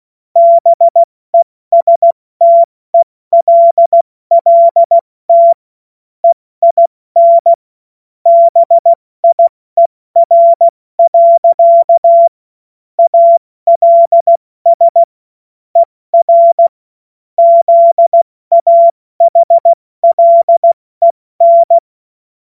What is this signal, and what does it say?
Morse code, heard as a keyed tone, about 16 wpm, 680 Hz. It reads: BESTELLT EIN BIER. ALS ER ZAHLEN
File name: Morsekurs / Witze / Witz 11. Mai 2021, 70 F80